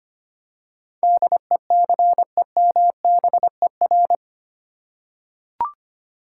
Morse code, heard as a keyed tone, about 25 wpm, 700 Hz. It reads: DECEMBER
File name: Eight Letter Words - 25wpm